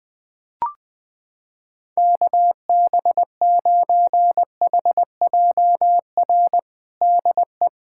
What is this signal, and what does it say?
Morse code, heard as a keyed tone, about 21 wpm, 700 Hz. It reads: KB9HJR DE
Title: USA Callsign DE Callsign - 6 Characters - 20wpm